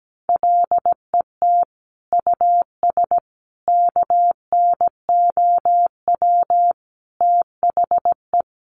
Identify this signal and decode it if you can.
Morse code, heard as a keyed tone, about 17 wpm, 700 Hz. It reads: LET US KNOW THE
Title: Sentences from Top 1200 Words 17wpm